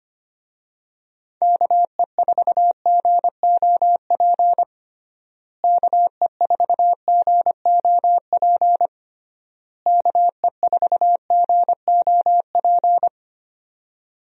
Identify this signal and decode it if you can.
Morse code, heard as a keyed tone, about 25 wpm, 700 Hz. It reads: KE4GOP KE4GOP KE4GOP
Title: USA Call Signs - 6 Characters - Repeated 3x - 25wpm